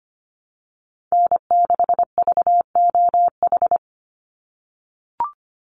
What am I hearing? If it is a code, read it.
Morse code, heard as a keyed tone, about 25 wpm, 700 Hz. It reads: N6VOH
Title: Call Signs - 25wpm